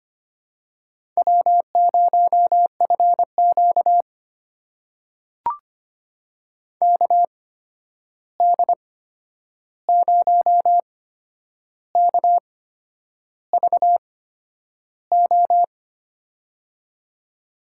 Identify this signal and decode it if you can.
Morse code, heard as a keyed tone, about 25 wpm, spaced as for 9 wpm, 700 Hz. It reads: W0FQ KD0KVO
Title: USA Call Signs - 8x Character Spacing - 25wpm